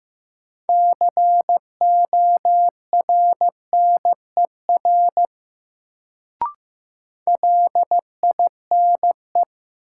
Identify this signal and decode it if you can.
Morse code, heard as a keyed tone, about 15 wpm, 700 Hz. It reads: CORNER LINE
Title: Sets of 5 Words - New Word Drill - 15wpm